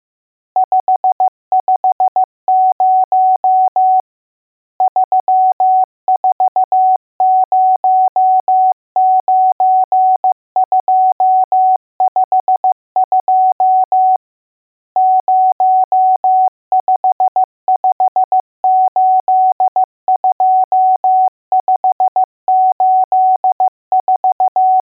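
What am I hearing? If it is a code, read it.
Morse code, heard as a keyed tone, about 15 wpm, 750 Hz. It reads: 550 3409252 05582584
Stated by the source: Morse code lesson: test on all digits.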